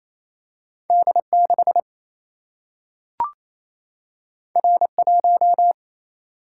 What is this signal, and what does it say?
Morse code, heard as a keyed tone, about 28 wpm, 700 Hz. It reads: D6 R1
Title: Single-Letter Single-Number 28wpm